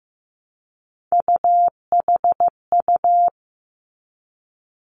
Morse code, heard as a keyed tone, about 15 wpm, 700 Hz. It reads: UHU
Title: Three Challenging Letter-Number Combinations 15wpm